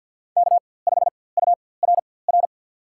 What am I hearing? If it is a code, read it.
Morse code, heard as a keyed tone, about 53 wpm, spaced as for 24 wpm, 700 Hz. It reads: K5URR
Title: USA Call Signs - 5 Characters - 4x Character Spacing - 50wpm